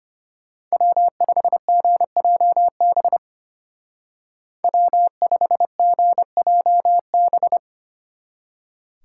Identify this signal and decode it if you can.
Morse code, heard as a keyed tone, about 30 wpm, 700 Hz. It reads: W5GJB W5GJB
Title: Speed Racing - USA 1x3 Call Signs 25wpm